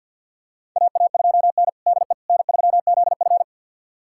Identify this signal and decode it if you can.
Morse code, heard as a keyed tone, about 52 wpm, 700 Hz. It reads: AA2N DE N3ZF
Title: USA Callsign DE Callsign - 4 Characters - 50wpm